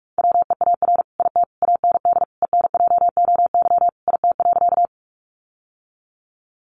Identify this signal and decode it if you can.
Morse code, heard as a keyed tone, about 44 wpm, 700 Hz. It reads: WEAR IT AND ENJOY IT.
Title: Sentences from Top 1500 Words 45wpm